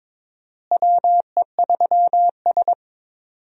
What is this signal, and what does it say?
Morse code, heard as a keyed tone, about 22 wpm, 700 Hz. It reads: WE3S
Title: USA Call Signs - 4 Characters - Repeated 2x - 22wpm